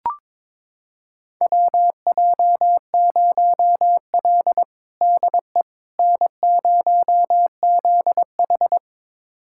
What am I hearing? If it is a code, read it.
Morse code, heard as a keyed tone, about 21 wpm, 700 Hz. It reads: WJ0L DE N0ZH
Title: USA Callsign DE Callsign - 4 Characters - 22wpm